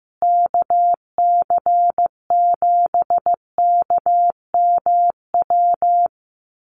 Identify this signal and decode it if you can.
Morse code, heard as a keyed tone, about 15 wpm, 700 Hz. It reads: KC7KMW
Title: USA Call Signs Repeated 3x 15wpm